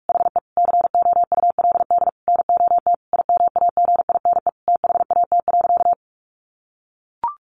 Morse code, heard as a keyed tone, about 44 wpm, 700 Hz. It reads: HE COULD NOT IMAGINE THAT.
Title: Sentences from Top 1400 Words 45wpm